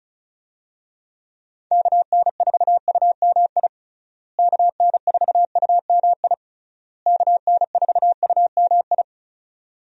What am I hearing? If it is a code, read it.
Morse code, heard as a keyed tone, about 34 wpm, 700 Hz. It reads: KN4UMI KN4UMI KN4UMI
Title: USA Call Signs - 6 Characters - Repeated 3x - 35wpm